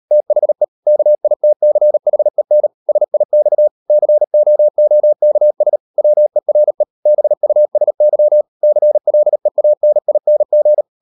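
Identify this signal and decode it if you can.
Morse code, heard as a keyed tone, about 38 wpm, 600 Hz. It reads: THE KITCHEN SIX COOKS WERE BUSY CLEANING
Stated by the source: Morse code practice at 38 WPM, sending Ken Follett's novel Never in English, from CW med Gnister.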